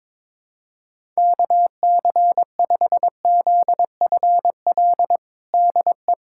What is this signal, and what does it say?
Morse code, heard as a keyed tone, about 22 wpm, 700 Hz. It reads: KC5ZFL DE
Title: USA Callsign DE Callsign - 6 Characters - 22wpm